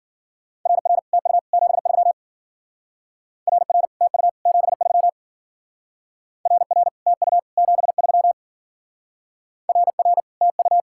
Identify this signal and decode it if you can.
Morse code, heard as a keyed tone, about 48 wpm, 700 Hz. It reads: RR TU 73 RR TU 73 RR TU 73 RR TU
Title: Speed-Racing Common QSO Phrases Encore - 40wpm